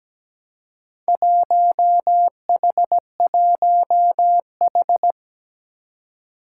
Morse code, heard as a keyed tone, about 17 wpm, 700 Hz. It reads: 1H1H
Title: Four Challenging Letter-Number Combinations 17wpm